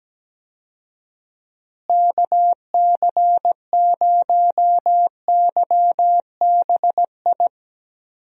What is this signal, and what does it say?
Morse code, heard as a keyed tone, about 17 wpm, 700 Hz. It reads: KC0YBI
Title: USA Call Signs - 6 Characters - Repeated 2x - 17wpm